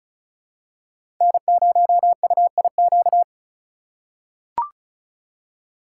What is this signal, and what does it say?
Morse code, heard as a keyed tone, about 34 wpm, 700 Hz. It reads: N0UIQ
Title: USA Call Signs - 5 Characters - 3x Character Spacing - 35wpm